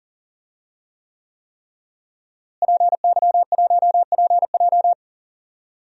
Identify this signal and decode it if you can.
Morse code, heard as a keyed tone, about 40 wpm, 700 Hz. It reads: PY1PJ
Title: DXCC-100 Call Signs - 5 Characters - Repeated 3x - 40wpm